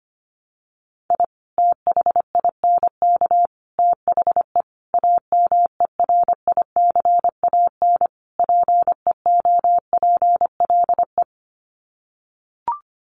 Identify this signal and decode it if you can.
Morse code, heard as a keyed tone, about 25 wpm, 700 Hz. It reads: I THINK THE AMERICAN PEOPLE
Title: Sets of 5 Words Encore 25wpm